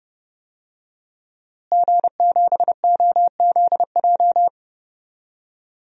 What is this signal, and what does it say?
Morse code, heard as a keyed tone, about 30 wpm, 700 Hz. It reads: G7OZJ
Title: DXCC-100 Call Signs - 5 Characters - Repeated 3x - 30wpm